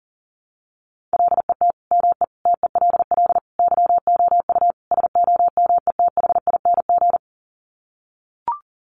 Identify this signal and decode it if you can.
Morse code, heard as a keyed tone, about 40 wpm, 700 Hz. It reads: LET ME TELL YOU SOMETHING
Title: Sets of 5 Words 40wpm